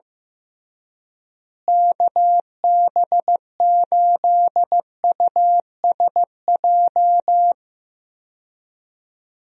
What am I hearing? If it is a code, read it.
Morse code, heard as a keyed tone, about 15 wpm, 700 Hz. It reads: KB8USJ
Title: USA Call Signs - 6 Characters - Repeated 3x - 15wpm